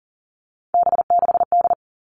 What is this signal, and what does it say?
Morse code, heard as a keyed tone, about 40 wpm, 700 Hz. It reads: B6D
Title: All Cut Numbers - 3-Digit 40wpm